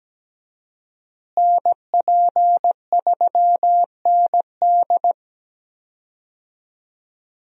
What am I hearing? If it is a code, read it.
Morse code, heard as a keyed tone, about 17 wpm, 700 Hz. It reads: NP3ND
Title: USA Call Signs - 5 Characters - Repeated 3x - 17wpm